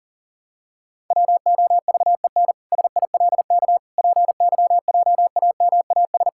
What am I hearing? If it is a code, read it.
Morse code, heard as a keyed tone, about 37 wpm, 700 Hz. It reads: WOVEN SILK PYJAMAS
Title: Pangram Challenge v2 - 40wpm